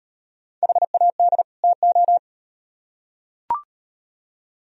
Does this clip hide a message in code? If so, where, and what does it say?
Yes: Morse code from 0.6 to 2.2 s, heard as a keyed tone, about 36 wpm, 700 Hz. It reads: HAD TO